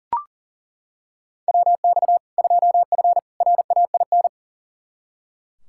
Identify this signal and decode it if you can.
Morse code, heard as a keyed tone, about 40 wpm, 700 Hz. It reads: WX 2F RAIN
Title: K1USN SST QSO Practice 40wpm